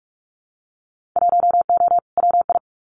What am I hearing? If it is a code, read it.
Morse code, heard as a keyed tone, about 46 wpm, 700 Hz. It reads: 1O WI
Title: Winter Field Day Exchanges 45wpm